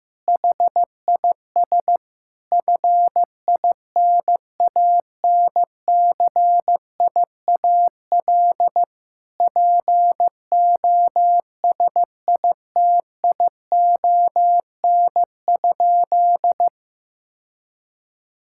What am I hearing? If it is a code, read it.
Morse code, heard as a keyed tone, about 15 wpm, 700 Hz. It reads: HIS FINANCIAL POSITION?
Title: Sentences from Top 1000 Words - Review Part 5 - 15wpm